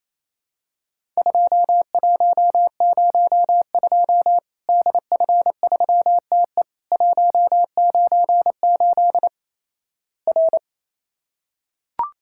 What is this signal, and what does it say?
Morse code, heard as a keyed tone, about 27 wpm, 700 Hz. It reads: 2102 DF3TE 198 R
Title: Worked All Europe DX Contest - QTC Exchange - 28wpm